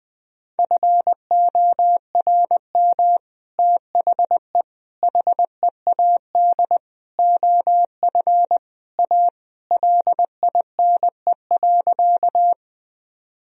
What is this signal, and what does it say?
Morse code, heard as a keyed tone, about 20 wpm, 700 Hz. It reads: FORM THE HEAD OF A LINE.
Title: Sentences from Top 1000 Words - Review Part 1 - 20wpm